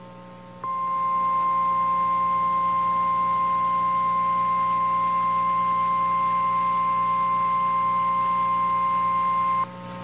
Emergency Alert System (SAME) tones (the 1050 Hz attention tone), with noise. It is heard from 0.6 to 9.7 s.
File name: eas / KNCYKBIE / KNCYKBIE